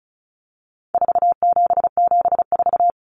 Morse code, heard as a keyed tone, about 34 wpm, 700 Hz. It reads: 4774
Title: Common Cut Numbers - 4-Digit 35wpm